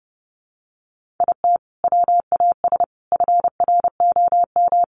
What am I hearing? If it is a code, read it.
Morse code, heard as a keyed tone, about 30 wpm, 700 Hz. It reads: IT WAS FROM
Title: Sentences from Top 100 Words 30wpm